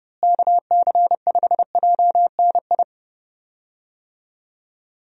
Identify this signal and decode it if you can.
Morse code, heard as a keyed tone, about 30 wpm, 700 Hz. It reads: KC5JNI
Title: USA Call Signs - 6 Characters - Repeated 3x - 30wpm